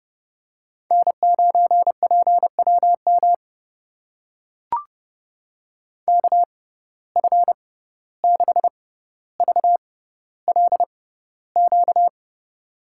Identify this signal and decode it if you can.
Morse code, heard as a keyed tone, about 29 wpm, spaced as for 15 wpm, 700 Hz. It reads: N9PWM KF6VLQ
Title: USA Call Signs - 6x Character Spacing - 30wpm